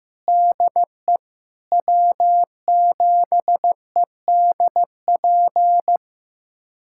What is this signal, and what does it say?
Morse code, heard as a keyed tone, about 15 wpm, 700 Hz. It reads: DE W7EDP
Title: USA Callsign DE Callsign 15wpm